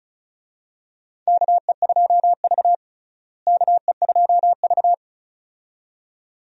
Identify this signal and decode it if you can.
Morse code, heard as a keyed tone, about 36 wpm, 700 Hz. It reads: KE2V KE2V
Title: USA Call Signs - 4 Characters - Repeated 2x - 35wpm